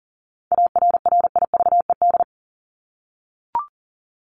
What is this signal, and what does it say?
Morse code, heard as a keyed tone, about 40 wpm, 700 Hz. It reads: ARRIVED
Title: ED Suffix - Top 100 Words with Base Words 40wpm